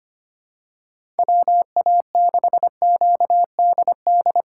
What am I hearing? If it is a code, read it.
Morse code, heard as a keyed tone, about 25 wpm, 700 Hz. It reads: WA6QDD
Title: USA Call Signs - 6 Characters - 4x Character Spacing - 25wpm